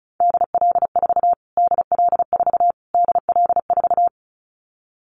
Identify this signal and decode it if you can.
Morse code, heard as a keyed tone, about 34 wpm, 700 Hz. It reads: DL4 DL4 DL4
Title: Callsign Prefixes for Top 25 DXCC - Repeated 3x - 35wpm